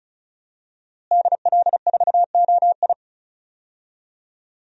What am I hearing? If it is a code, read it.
Morse code, heard as a keyed tone, about 34 wpm, 700 Hz. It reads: DL4OI
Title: USA Call Signs - 5 Characters - Repeated 2x - 35wpm